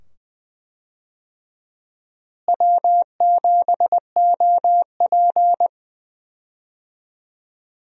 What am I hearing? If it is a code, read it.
Morse code, heard as a keyed tone, about 20 wpm, 700 Hz. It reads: W7OP
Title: USA Call Signs - 4 Characters - Repeated 2x - 20wpm